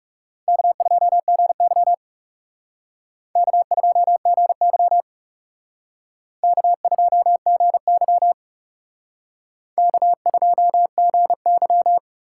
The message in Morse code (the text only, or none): K2GY K2GY K2GY K2GY